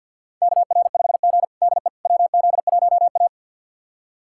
Morse code, heard as a keyed tone, about 46 wpm, 700 Hz. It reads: KR5G DE WZ1A
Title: USA Callsign DE Callsign - 4 Characters - 50wpm